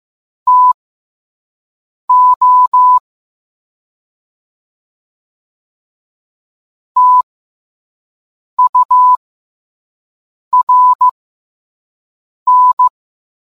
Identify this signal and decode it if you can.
Morse code, heard as a keyed tone, about 15 wpm, spaced as for 5 wpm, 1000 Hz. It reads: TO TURN